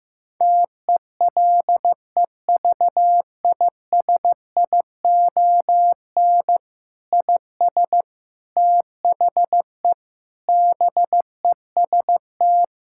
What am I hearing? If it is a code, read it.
Morse code, heard as a keyed tone, about 15 wpm, 700 Hz. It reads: TELEVISION IS THE BEST
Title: Sentences from Top 1000 Words - Review Part 6 - 15wpm